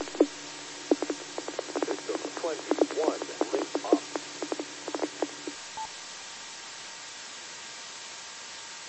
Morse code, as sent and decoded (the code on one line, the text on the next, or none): none